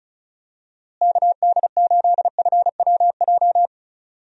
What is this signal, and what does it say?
Morse code, heard as a keyed tone, about 36 wpm, 700 Hz. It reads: KD8FWJ
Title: USA Call Signs - 6 Characters - Repeated 3x - 35wpm